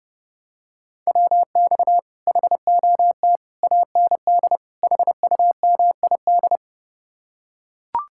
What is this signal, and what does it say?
Morse code, heard as a keyed tone, about 30 wpm, 700 Hz. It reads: WX HOT AND HUMID